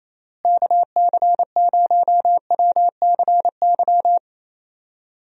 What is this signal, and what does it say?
Morse code, heard as a keyed tone, about 28 wpm, 700 Hz. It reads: KC0WCY
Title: USA Call Signs - 6 Characters - Repeated 3x - 28wpm